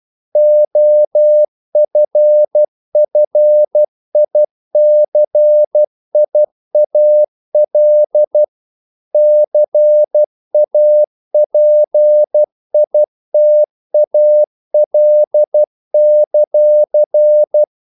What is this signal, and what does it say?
Morse code, heard as a keyed tone, about 12 wpm, 600 Hz. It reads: OFFICIAL CAPITAL;
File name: CW / Practice / NationalCapitals12